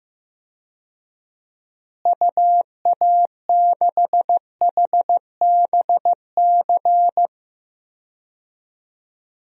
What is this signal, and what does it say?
Morse code, heard as a keyed tone, about 15 wpm, 700 Hz. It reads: UA6HBC
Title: DXCC-100 Call Signs - 6 Characters - Repeated 2x - 15wpm